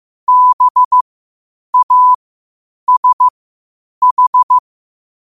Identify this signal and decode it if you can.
Morse code, heard as a keyed tone, about 15 wpm, spaced as for 8 wpm, 1000 Hz. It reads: BASH
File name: qotdfiles / 8wpm